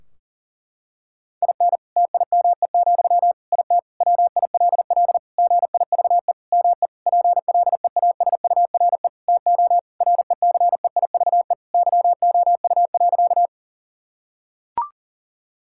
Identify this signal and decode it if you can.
Morse code, heard as a keyed tone, about 40 wpm, 700 Hz. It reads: IN TIME, IT WILL GIVE ME PLEASURE TO RECEIVE YOU.
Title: Sentences from Top 1600 Words 40wpm